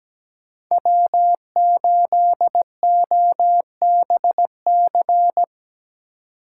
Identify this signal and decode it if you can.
Morse code, heard as a keyed tone, about 20 wpm, 700 Hz. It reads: W8OBC
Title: Speed Racing - USA Call Signs 17wpm